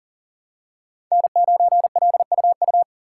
Morse code, heard as a keyed tone, about 40 wpm, 700 Hz. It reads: N9LUU
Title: USA Call Signs - 5 Characters - 3x Character Spacing - 40wpm